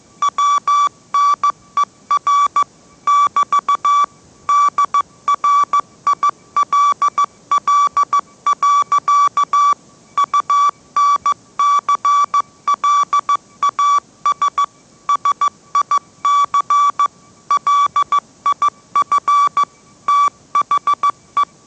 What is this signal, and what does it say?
Morse code, heard as a keyed tone, about 15 wpm, 1150 Hz. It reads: WNER <BT> DRILL. UNCLAS SIC LIF THE
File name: mrx / mrx065 16wpm